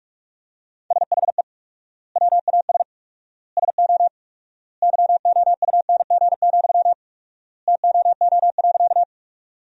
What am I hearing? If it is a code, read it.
Morse code, heard as a keyed tone, about 44 wpm, 700 Hz. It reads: SHE WAS SO YOUNG, TOO.
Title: 4x Word-Spacing - Sentences from Top 400 Words - 45wpm